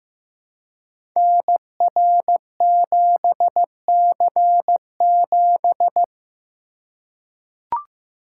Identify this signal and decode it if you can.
Morse code, heard as a keyed tone, about 15 wpm, 700 Hz. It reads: NR7C7
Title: Code Groups - 5 Letter:Numbers - 15wpm